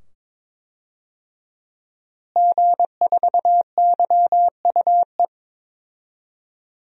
Morse code, heard as a keyed tone, about 22 wpm, 700 Hz. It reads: G4YUE
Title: DXCC-100 Call Signs - 5 Characters - Repeated 2x - 22wpm